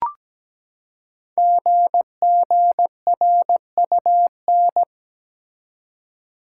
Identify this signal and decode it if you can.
Morse code, heard as a keyed tone, about 17 wpm, 700 Hz. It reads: GGRUN
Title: Code Groups - 5 Letters - 17wpm